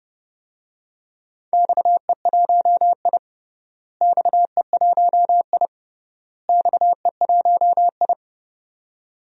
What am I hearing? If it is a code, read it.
Morse code, heard as a keyed tone, about 28 wpm, 700 Hz. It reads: XE1I XE1I XE1I
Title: DXCC-100 Call Signs - 4 Characters - Repeated 3x - 30wpm